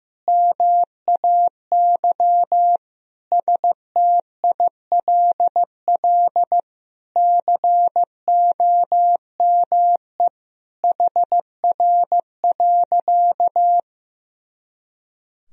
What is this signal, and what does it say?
Morse code, heard as a keyed tone, about 16 wpm, 700 Hz. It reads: MAY STILL COME HR.
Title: Sentences from Top 300 Words and Abbreviations 15wpm